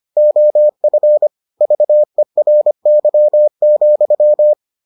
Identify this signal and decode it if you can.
Morse code, heard as a keyed tone, about 25 wpm, 600 Hz. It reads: OF VERY,